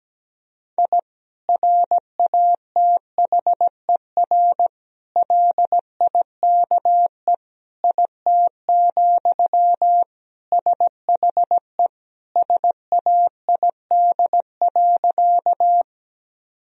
Morse code, heard as a keyed tone, about 17 wpm, 700 Hz. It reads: I RATHER LIKE IT, SHE SAID.
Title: Sentences from Top 1000 Words - Review Part 4 - 17wpm